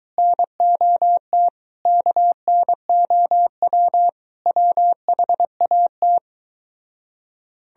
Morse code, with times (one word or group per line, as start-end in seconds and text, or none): 0.2-1.5 NOT
1.9-4.1 KNOW
4.5-6.2 WHAT